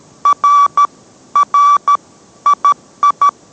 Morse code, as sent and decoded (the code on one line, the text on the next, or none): .-. / .-. / .. ..
R R II